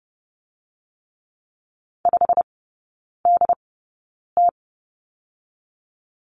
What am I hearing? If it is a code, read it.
Morse code, heard as a keyed tone, about 30 wpm, spaced as for 9 wpm, 700 Hz. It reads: 5DT
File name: Lesson 17e - Letter u - Three Characters 30wpm